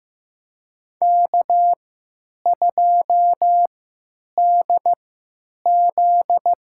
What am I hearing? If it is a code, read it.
Morse code, heard as a keyed tone, about 15 wpm, spaced as for 8 wpm, 700 Hz. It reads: K2DZ
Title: USA Call Signs - 4 Characters - 3x Character Spacing - 15wpm